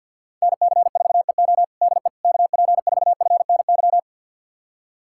Morse code, heard as a keyed tone, about 46 wpm, 700 Hz. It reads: NC4EO DE KP4FNY
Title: USA Callsign DE Callsign 50wpm